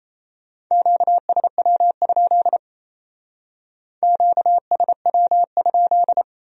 Morse code, heard as a keyed tone, about 35 wpm, 700 Hz. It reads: QSW? QSW?
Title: Speed-Racing Q-Codes - 28wpm